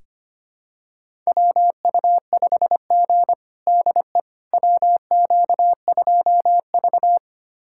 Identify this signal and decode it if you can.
Morse code, heard as a keyed tone, about 25 wpm, 700 Hz. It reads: WU5G DE WQ2V
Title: USA Callsign DE Callsign - 4 Characters - 25wpm